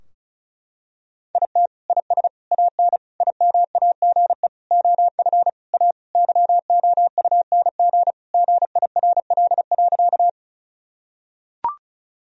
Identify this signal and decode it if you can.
Morse code, heard as a keyed tone, about 36 wpm, 700 Hz. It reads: IT IS AN IMAGE OF A YOUNG GIRL.
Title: Sentences from Top 1000 Words - Review Part 4 - 35wpm